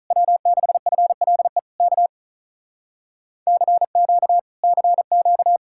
Morse code, heard as a keyed tone, about 47 wpm, 700 Hz. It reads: W6FLE K CQ CQ